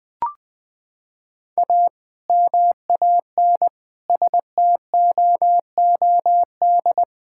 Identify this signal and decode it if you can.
Morse code, heard as a keyed tone, about 20 wpm, 700 Hz. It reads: A MAN STOOD
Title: Sentences with 2-Word Phrases - 20wpm